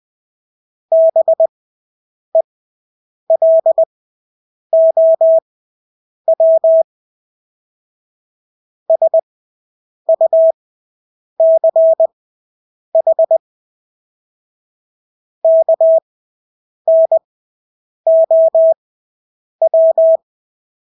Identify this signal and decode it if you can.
Morse code, heard as a keyed tone, about 20 wpm, spaced as for 8 wpm, 650 Hz. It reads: BELOW SUCH KNOW
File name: Engelske ord 20/8 wpm